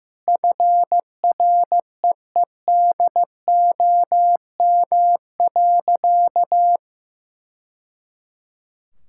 Morse code, heard as a keyed tone, about 15 wpm, 700 Hz. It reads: FREEDOM.
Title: Sentences from Top 1000 Words - Review Part 5 - 15wpm